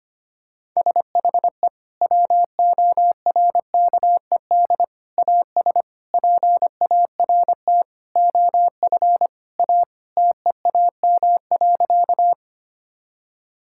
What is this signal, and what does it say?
Morse code, heard as a keyed tone, about 25 wpm, 700 Hz. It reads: SHE WORKED AS PART OF A TEAM.
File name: practice / 25wpm / Sentences with 4-Word Phrases - 25wpm